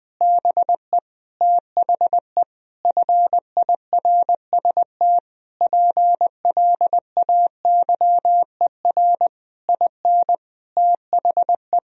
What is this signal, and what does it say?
Morse code, heard as a keyed tone, about 21 wpm, 700 Hz. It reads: BE THE FIRST PLAYER IN THE
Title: Sentences from Top 1000 Words - Review Part 4 - 20wpm